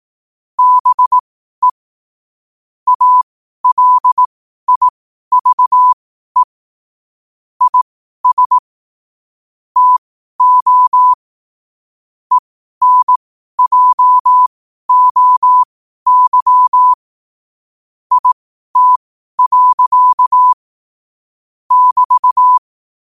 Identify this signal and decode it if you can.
Morse code, heard as a keyed tone, about 18 wpm, spaced as for 12 wpm, 1000 Hz. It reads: BE ALIVE IS TO ENJOY IT. <BT>
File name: qotdfiles / 12wpm